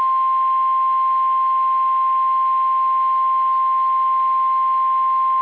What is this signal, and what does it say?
Emergency Alert System (SAME) tones (the 1050 Hz attention tone). It is heard from 0.8 to 5.2 s.
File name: eas / KSDZKDJL / KSDZKDJL